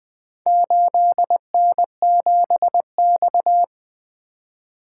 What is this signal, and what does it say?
Morse code, heard as a keyed tone, about 20 wpm, 700 Hz. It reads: 8N7X
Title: Four Letter-Number Combinations - 20wpm